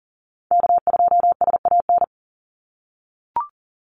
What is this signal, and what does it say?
Morse code, heard as a keyed tone, about 38 wpm, 700 Hz. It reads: K2SAN
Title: Call Signs - 40wpm